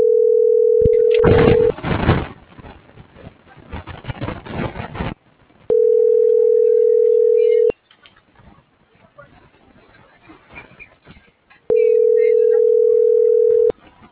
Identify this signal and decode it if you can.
Ringback tone (North America), 440 + 480 Hz.